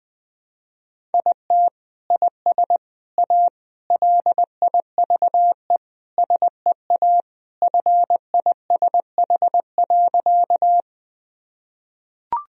Morse code, heard as a keyed tone, about 20 wpm, 700 Hz. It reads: IT IS A LIVE SEA FISH.
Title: Sentences from Top 1000 Words - Review Part 1 - 20wpm